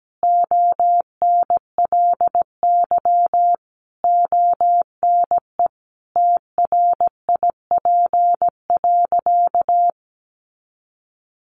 Morse code, heard as a keyed tone, about 17 wpm, 700 Hz. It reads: ONLY ONE TRIP.
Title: Sentences from Top 1300 Words 17wpm